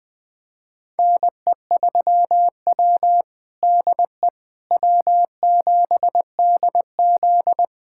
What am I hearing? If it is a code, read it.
Morse code, heard as a keyed tone, about 20 wpm, 700 Hz. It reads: NE3W DE W7DZ
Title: USA Callsign DE Callsign - 4 Characters - 20wpm